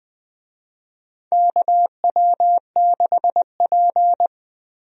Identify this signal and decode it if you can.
Morse code, heard as a keyed tone, about 20 wpm, 700 Hz. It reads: KW6P
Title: USA Call Signs - 4 Characters - Repeated 2x - 20wpm